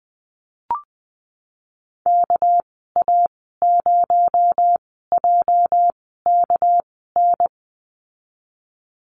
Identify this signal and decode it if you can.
Morse code, heard as a keyed tone, about 20 wpm, spaced as for 14 wpm, 700 Hz. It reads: KA0JKN
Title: 219 Lesson 35g - Number 0 - Call Signs 20wpm